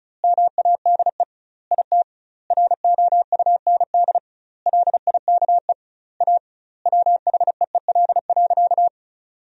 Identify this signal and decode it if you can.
Morse code, heard as a keyed tone, about 36 wpm, 700 Hz. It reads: MADE IT ROUND LIKE A WHEEL.
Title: 2x Word-Spacing - Sentences from Top 500 Words - 35wpm